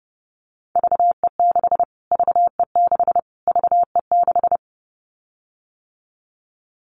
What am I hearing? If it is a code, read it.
Morse code, heard as a keyed tone, about 30 wpm, 700 Hz. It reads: VE6 VE6 VE6
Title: Callsign Prefixes for Top 25 DXCC - Repeated 3x - 30wpm